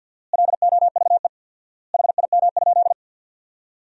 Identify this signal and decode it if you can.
Morse code, heard as a keyed tone, about 48 wpm, 700 Hz. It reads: LOVE HIM?
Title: 4x Word-Spacing - Sentences from Top 400 Words - 50wpm